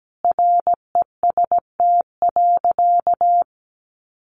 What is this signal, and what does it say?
Morse code, heard as a keyed tone, about 17 wpm, 700 Hz. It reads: REST.
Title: Sentences from Top 500 Words 17wpm